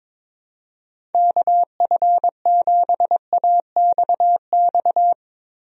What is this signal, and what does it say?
Morse code, heard as a keyed tone, about 21 wpm, 700 Hz. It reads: KF7AXX
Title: USA Call Signs - 6 Characters - 22wpm